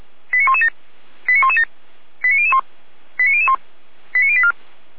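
Five-tone selective call (ZVEI), heard as quick runs of five tones, tones of about 70 ms. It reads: 89117 89117 89001 89001 89084
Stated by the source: Received on an SDR; ZVEI five-tone alert from the server of Feuerwehr Orlamünde (f85360).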